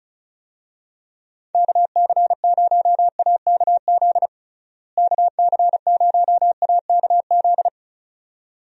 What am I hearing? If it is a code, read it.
Morse code, heard as a keyed tone, about 34 wpm, 700 Hz. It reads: KC0AKZ KC0AKZ
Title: USA Call Signs - 6 Characters - Repeated 2x - 35wpm